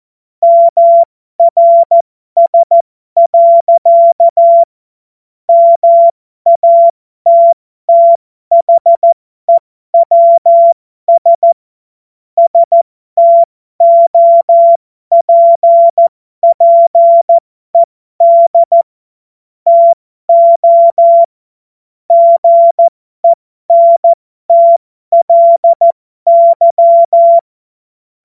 Morse code, heard as a keyed tone, about 14 wpm, spaced as for 12 wpm, 680 Hz. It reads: MRS. MATTHEWS STOPPED TO GENTLY
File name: Morsekurs / Witze